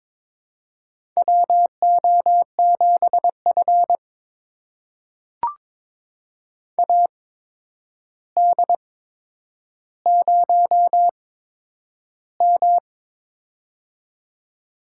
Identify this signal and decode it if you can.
Morse code, heard as a keyed tone, about 22 wpm, spaced as for 9 wpm, 700 Hz. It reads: WO7F AD0M
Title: USA Call Signs - 4 Characters - 8x Character Spacing - 22wpm